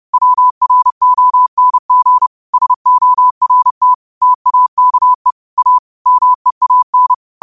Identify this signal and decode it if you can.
Morse code, heard as a keyed tone, about 30 wpm, 1000 Hz. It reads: WRONG SORT TAKE A MEAN
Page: New quotes every day in morse code at 30 Words per minute.